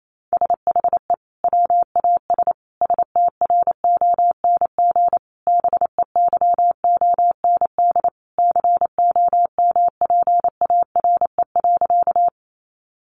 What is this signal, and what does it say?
Morse code, heard as a keyed tone, about 28 wpm, 700 Hz. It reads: SHE WAS STRONG BEYOND COMPARE.
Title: Sentences from Top 1300 Words 28wpm